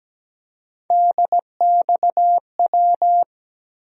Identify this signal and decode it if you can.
Morse code, heard as a keyed tone, about 17 wpm, 700 Hz. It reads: DXW
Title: USA Call Sign Postfixes - 3 Letters 17wpm